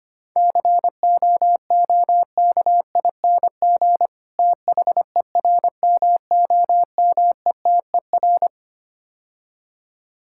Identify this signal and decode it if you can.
Morse code, heard as a keyed tone, about 25 wpm, 700 Hz. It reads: COOKING THERMOMETER
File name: practice / 25wpm / Difficult Words - Speed-Racing with Hints - 25wpm